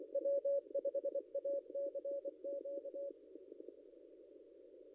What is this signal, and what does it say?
Morse code, heard as a keyed tone, about 22 wpm, 560 Hz. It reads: W5ACQ
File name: contests / ss-cw-k5tr-op-n5rz -0800